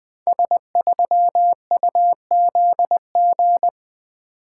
Morse code, heard as a keyed tone, about 20 wpm, 700 Hz. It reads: S3UZG
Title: Code Groups - 5 Letter:Numbers - 20wpm